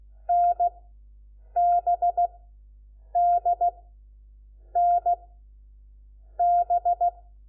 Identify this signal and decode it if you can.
Morse code, heard as a keyed tone, about 16 wpm, spaced as for 7 wpm, 700 Hz. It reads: NBDNB
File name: Lecciones / CLASE Nº4